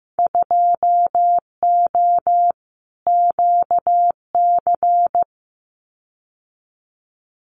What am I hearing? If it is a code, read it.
Morse code, heard as a keyed tone, about 15 wpm, 700 Hz. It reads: 2O QC
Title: Winter Field Day Exchanges 15wpm